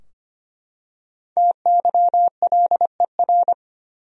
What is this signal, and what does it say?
Morse code, heard as a keyed tone, about 25 wpm, 700 Hz. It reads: TYLER
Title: USA Common Names - Max 5 Letters - 25wpm